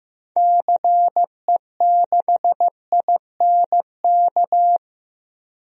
Morse code, heard as a keyed tone, about 15 wpm, 700 Hz. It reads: CE6INK
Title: DXCC-100 Call Signs - 6 Characters - Repeated 2x - 15wpm